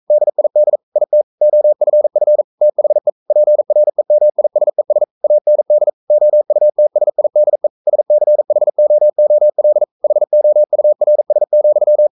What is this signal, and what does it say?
Morse code, heard as a keyed tone, about 42 wpm, 600 Hz. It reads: DID IT OFF THE PREMISES AND OUTSIDE SCHOOL HOURS,
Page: Romanen Never af Ken Follet på engelsk.